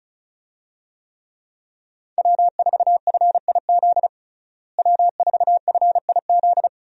Morse code, heard as a keyed tone, about 34 wpm, 700 Hz. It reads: W4FIZ W4FIZ
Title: USA Call Signs - 5 Characters - Repeated 2x - 35wpm